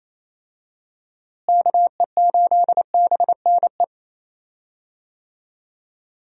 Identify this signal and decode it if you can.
Morse code, heard as a keyed tone, about 28 wpm, 700 Hz. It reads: KE8BNE
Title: USA Call Signs - 6 Characters - Repeated 2x - 28wpm